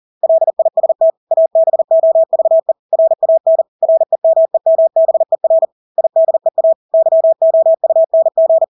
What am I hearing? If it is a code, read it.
Morse code, heard as a keyed tone, about 40 wpm, 650 Hz. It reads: LIST ABOVE RAN REMEMBER IDEA YOUNG
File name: Engelske ord 40-40 wpm